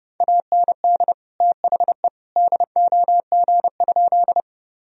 Morse code, heard as a keyed tone, about 32 wpm, 700 Hz. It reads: AND THE DOG?
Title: Sentences from Top 1000 Words - Review Part 2 - 30wpm